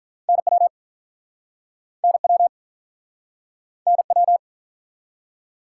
Morse code, heard as a keyed tone, about 43 wpm, 700 Hz. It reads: NW NW NW
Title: Speed-Racing Top 180 QSO Elements 35wpm